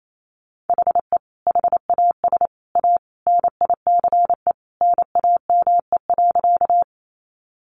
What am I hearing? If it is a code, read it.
Morse code, heard as a keyed tone, about 28 wpm, 700 Hz. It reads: HE HAS A NICE NAME.
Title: Sentences from Top 1100 Words 28wpm